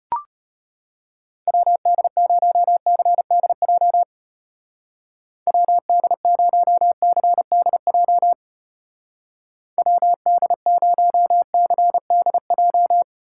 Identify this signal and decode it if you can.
Morse code, heard as a keyed tone, about 31 wpm, 700 Hz. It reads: WD0CDJ WD0CDJ WD0CDJ
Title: Speed Racing - USA Call Signs 25wpm